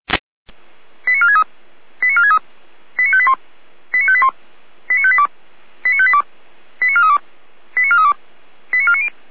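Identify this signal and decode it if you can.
Five-tone selective call (ZVEI), heard as quick runs of five tones, tones of about 80 ms. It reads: 89463 89463 89581 89581 89582 89582 89432 89432 89490
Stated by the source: ZVEI five-tone selective call received on an SDR.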